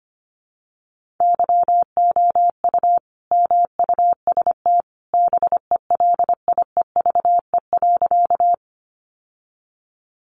Morse code, heard as a keyed tone, about 25 wpm, 700 Hz. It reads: YOU MUST BELIEVE.
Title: Sentences from Top 1000 Words 25wpm